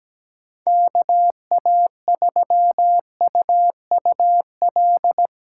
Morse code, heard as a keyed tone, about 17 wpm, 700 Hz. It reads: KA3UUL
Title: USA Call Signs - 6 Characters - 4x Character Spacing - 17wpm